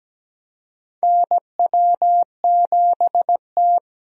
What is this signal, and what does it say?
Morse code, heard as a keyed tone, about 17 wpm, 700 Hz. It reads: NW7T
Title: USA Call Signs - 4 Characters - 17wpm